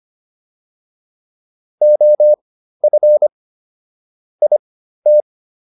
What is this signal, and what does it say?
Morse code, heard as a keyed tone, about 25 wpm, spaced as for 13 wpm, 600 Hz. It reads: OF IT